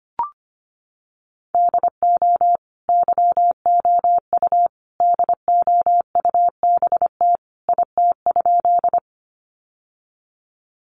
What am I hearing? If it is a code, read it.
Morse code, heard as a keyed tone, about 25 wpm, 700 Hz. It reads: DO YOU DOUBT IT?
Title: Sentences from Top 1500 Words 25wpm